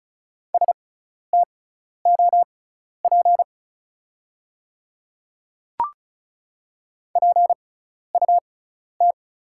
Morse code, heard as a keyed tone, about 36 wpm, spaced as for 12 wpm, 690 Hz. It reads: STOP PUT
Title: 6x Character Spacing - Top 500 Words - 35wpm